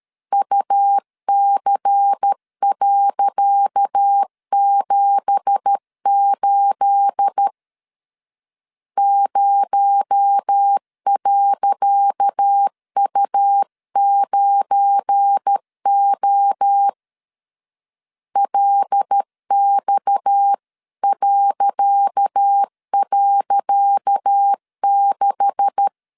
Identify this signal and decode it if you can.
Morse code, heard as a keyed tone, about 13 wpm, 800 Hz. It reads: UC.78 0.U9O LX..6
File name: LEZIONI / 10 SIGNS / 10C DOT 12wpm 2451